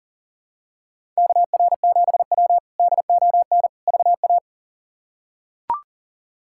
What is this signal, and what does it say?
Morse code, heard as a keyed tone, about 40 wpm, 700 Hz. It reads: KR7W DON VA
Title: K1USN SST QSO Practice - 40wpm